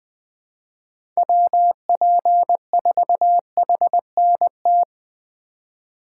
Morse code, heard as a keyed tone, about 20 wpm, 700 Hz. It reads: WP4HNT
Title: USA Call Signs - 6 Characters - 4x Character Spacing - 20wpm